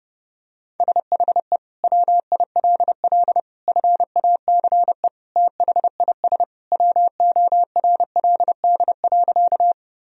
Morse code, heard as a keyed tone, about 30 wpm, 700 Hz. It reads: SHE WILL FACE THIS WORLD.
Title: Sentences from Top 1000 Words - Review Part 2 - 30wpm